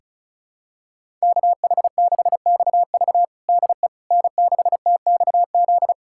Morse code, heard as a keyed tone, about 36 wpm, 700 Hz. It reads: KH6XV DE N6TXZ
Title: USA Callsign DE Callsign - 5 Characters - 35wpm